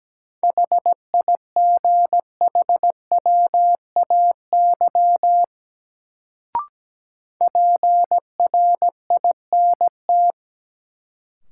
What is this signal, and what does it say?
Morse code, heard as a keyed tone, about 17 wpm, 700 Hz. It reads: HIGHWAY PRINT
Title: Sentences from Top 1400 Words - New Word Drill Alt - 17wpm